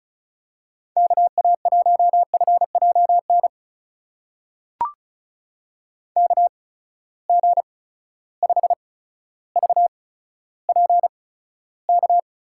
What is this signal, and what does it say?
Morse code, heard as a keyed tone, about 36 wpm, spaced as for 15 wpm, 700 Hz. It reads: KA1FJN KG5VPK
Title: USA Call Signs - 6 Characters - 8x Character Spacing - 35wpm